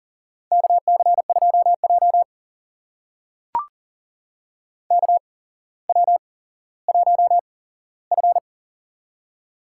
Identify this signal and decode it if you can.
Morse code, heard as a keyed tone, about 40 wpm, spaced as for 17 wpm, 700 Hz. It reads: KC2J KW1F
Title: USA Call Signs - 4 Characters - 8x Character Spacing - 40wpm